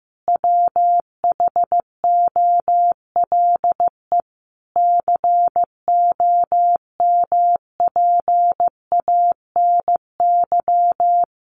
Morse code, heard as a keyed tone, about 15 wpm, 700 Hz. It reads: WHOLE COMPANY
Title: Sentences from Top 1000 Words 15wpm